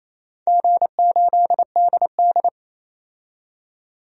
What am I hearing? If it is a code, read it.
Morse code, heard as a keyed tone, about 27 wpm, 690 Hz. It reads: G8DD
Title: DXCC-100 Call Signs - 4 Characters - 28wpm